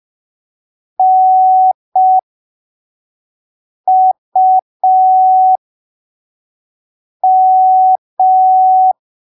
Morse code, heard as a keyed tone, about 5 wpm, spaced as for 3 wpm, 750 Hz. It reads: NUM
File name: arquivos / texto / texto 08 05ppm